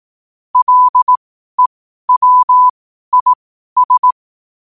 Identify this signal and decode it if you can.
Morse code, heard as a keyed tone, about 18 wpm, spaced as for 12 wpm, 1000 Hz. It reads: LEWIS
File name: qotdfiles / 12wpm